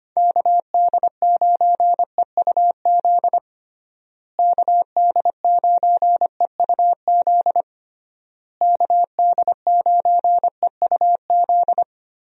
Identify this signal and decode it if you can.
Morse code, heard as a keyed tone, about 25 wpm, 700 Hz. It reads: KD9EUZ KD9EUZ KD9EUZ
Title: USA Call Signs - 6 Characters - Repeated 3x - 25wpm